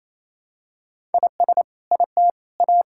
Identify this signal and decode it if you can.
Morse code, heard as a keyed tone, about 28 wpm, 700 Hz. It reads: IS IT A
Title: Sentences from Top 1000 Words - Review Part 6 - 28wpm